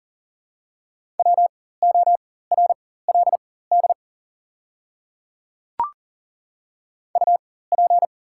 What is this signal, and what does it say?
Morse code, heard as a keyed tone, about 40 wpm, spaced as for 19 wpm, 700 Hz. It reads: WORLD UP
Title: 4x Character Spacing - Top 200 Words - 40wpm